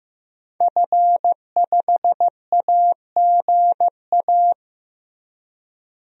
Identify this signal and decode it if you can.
Morse code, heard as a keyed tone, about 16 wpm, 700 Hz. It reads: F5AGA
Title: DXCC-100 Call Signs - 5 Characters - Repeated 2x - 15wpm